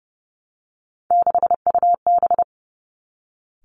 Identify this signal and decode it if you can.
Morse code, heard as a keyed tone, about 30 wpm, 700 Hz. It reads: 6UB
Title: Three Challenging Letter-Number Combinations 30wpm